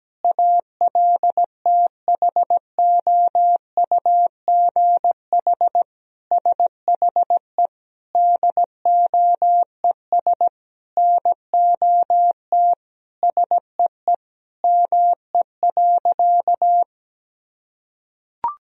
Morse code, heard as a keyed tone, about 17 wpm, 700 Hz. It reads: ALTHOUGH SHE DOES NOT SEE ME.
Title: Sentences from Top 1000 Words - Review Part 3 - 17wpm